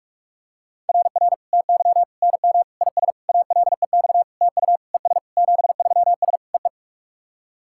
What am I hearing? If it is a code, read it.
Morse code, heard as a keyed tone, about 46 wpm, 690 Hz. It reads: RR TY NM IS ALEX TU ES 73S EE
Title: SOTA QSO Practice 45wpm